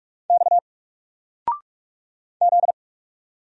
Morse code, heard as a keyed tone, about 44 wpm, 700 Hz. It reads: X Z
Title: Single Letter-Number 45wpm